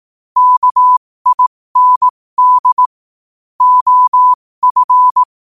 Morse code, heard as a keyed tone, about 18 wpm, spaced as for 15 wpm, 1000 Hz. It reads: KIND OF
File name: qotdfiles / 15wpm